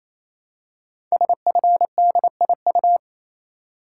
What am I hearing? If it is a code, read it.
Morse code, heard as a keyed tone, about 28 wpm, 700 Hz. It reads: SFDIU
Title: Code Groups - 5 Letters - 28wpm